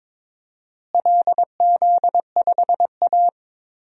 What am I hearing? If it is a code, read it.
Morse code, heard as a keyed tone, about 22 wpm, 700 Hz. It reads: LZ5A
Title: DXCC-100 Call Signs - 4 Characters - Repeated 2x - 22wpm